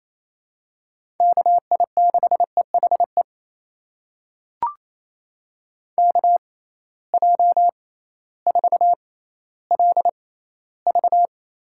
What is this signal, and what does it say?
Morse code, heard as a keyed tone, about 28 wpm, spaced as for 15 wpm, 700 Hz. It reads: KI6EHE KJ4LV
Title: USA Call Signs - 6 Characters - 6x Character Spacing - 28wpm